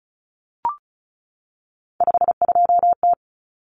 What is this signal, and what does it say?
Morse code, heard as a keyed tone, about 34 wpm, 700 Hz. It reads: 52T
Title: Common Cut Numbers - 3-Digit 35wpm